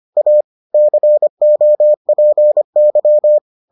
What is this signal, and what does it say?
Morse code, heard as a keyed tone, about 25 wpm, 600 Hz. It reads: A COPY